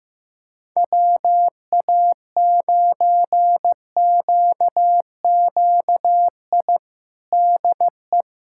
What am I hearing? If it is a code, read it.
Morse code, heard as a keyed tone, about 15 wpm, 700 Hz. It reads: WA9QQI DE
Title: USA Callsign DE Callsign 15wpm